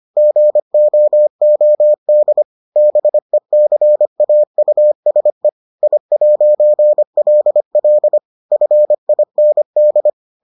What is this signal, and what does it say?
Morse code, heard as a keyed tone, about 25 wpm, 600 Hz. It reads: GOOD BECAUSE I'LL FIND